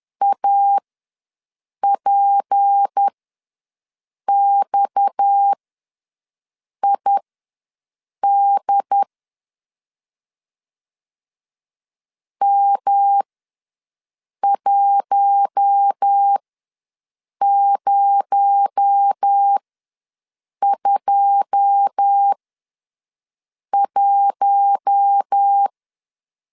Morse code, heard as a keyed tone, about 10 wpm, spaced as for 5 wpm, 800 Hz. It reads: APXID M1021
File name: LEZIONI / 9 2 ws 10wpm 4130